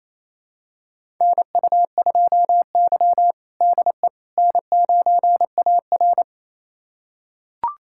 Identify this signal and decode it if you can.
Morse code, heard as a keyed tone, about 28 wpm, 700 Hz. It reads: NU2Y DE N9AR
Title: USA Callsign DE Callsign - 4 Characters - 28wpm